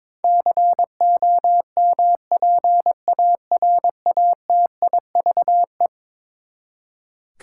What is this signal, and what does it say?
Morse code, heard as a keyed tone, about 22 wpm, 700 Hz. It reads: COMPARATIVE
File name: practice / 22wpm / Eleven Letter Words - 22wpm